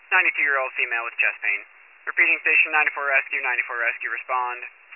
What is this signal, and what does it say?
Emergency Alert System (SAME) tones, heard as harsh buzzing bursts, with speech. It is heard from 0.2 to 4.2 s.